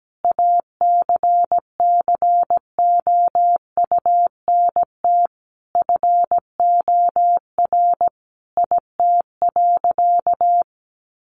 Morse code, heard as a keyed tone, about 17 wpm, 700 Hz. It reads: ACCOUNT FOR IT.
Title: Sentences from Top 900 Words 17wpm